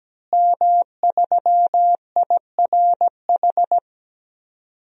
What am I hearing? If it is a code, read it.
Morse code, heard as a keyed tone, about 17 wpm, 700 Hz. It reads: M3IRH
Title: DXCC-100 Call Signs - 5 Characters - Repeated 2x - 17wpm